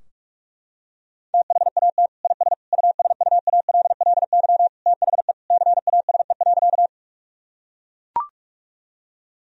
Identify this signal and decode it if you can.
Morse code, heard as a keyed tone, about 41 wpm, 700 Hz. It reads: THAT IS USUALLY THE CASE.
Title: Sentences from Top 1000 Words - Review Part 3 - 45wpm